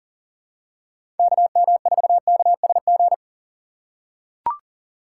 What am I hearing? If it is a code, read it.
Morse code, heard as a keyed tone, about 40 wpm, 700 Hz. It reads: KM4KSG
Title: USA Call Signs - 6x Character Spacing - 40wpm